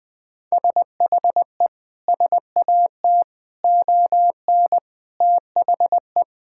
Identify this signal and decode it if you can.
Morse code, heard as a keyed tone, about 20 wpm, 700 Hz. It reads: SHE SAT ON THE
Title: Sentences with 6-Word Phrases - 20wpm